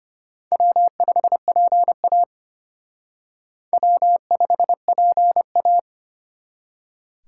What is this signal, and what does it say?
Morse code, heard as a keyed tone, about 32 wpm, 700 Hz. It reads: W5PA W5PA